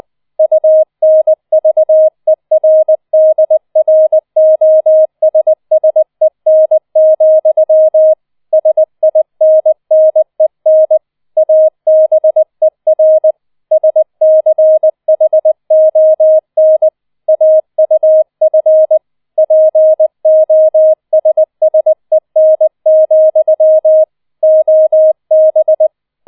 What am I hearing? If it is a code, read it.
Morse code, heard as a keyed tone, about 20 wpm, 630 Hz. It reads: UNVERDROSSEN, SINNEN ABER SCHON AUF POSSEN, OB